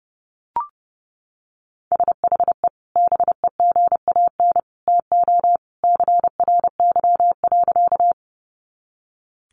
Morse code, heard as a keyed tone, about 30 wpm, 700 Hz. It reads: SHE BEGAN TO CRY.
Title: Sentences from Top 500 Words 30wpm